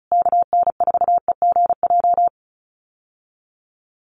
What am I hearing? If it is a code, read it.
Morse code, heard as a keyed tone, about 34 wpm, 700 Hz. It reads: KN4EGJ
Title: USA Call Signs Repeated 3x 35wpm